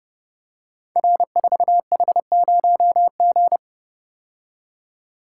Morse code, heard as a keyed tone, about 30 wpm, 700 Hz. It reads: R4H0G
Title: Code Groups - 5 Letter:Numbers - 30wpm